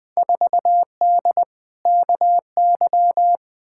Morse code, heard as a keyed tone, about 20 wpm, 700 Hz. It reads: 4D KY
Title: Field Day Exchanges Encore - 20wpm